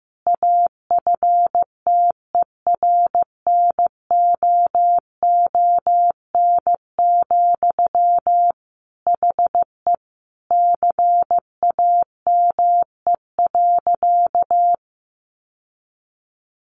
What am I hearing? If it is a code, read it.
Morse code, heard as a keyed tone, about 14 wpm, 700 Hz. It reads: AFTERNOON, HE CAME.
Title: Sentences from Top 1500 Words 15wpm